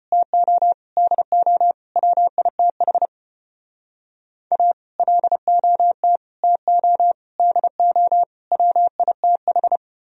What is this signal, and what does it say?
Morse code, heard as a keyed tone, about 33 wpm, 700 Hz. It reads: TO DO WITH A LOT TO DO WITH